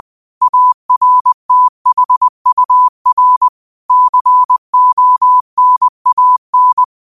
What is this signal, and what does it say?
Morse code, heard as a keyed tone, about 20 wpm, 1000 Hz. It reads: ARTHUR CONAN
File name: qotdfiles / 20wpm